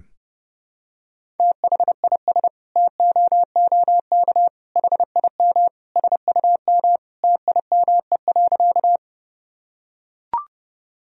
Morse code, heard as a keyed tone, about 31 wpm, 700 Hz. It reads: THIS TOOK HIM SUM TIME.
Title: Sentences from Top 300 Words and Abbreviations 30wpm